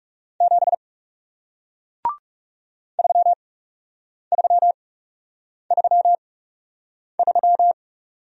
Morse code, heard as a keyed tone, about 40 wpm, spaced as for 8 wpm, 700 Hz. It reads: 7 3333